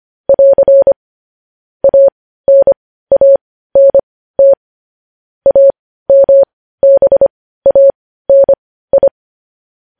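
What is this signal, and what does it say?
Morse code, heard as a keyed tone, about 25 wpm, spaced as for 15 wpm, 550 Hz. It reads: <AR> ANANT AMBANI
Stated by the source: Morse code transcription of news headlines.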